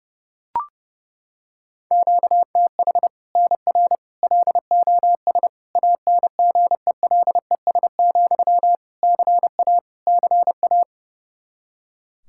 Morse code, heard as a keyed tone, about 32 wpm, 700 Hz. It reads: QTH NR LOS ANGELES, CA CA
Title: Practice QSO - Exchange 1 - Beginning - 30wpm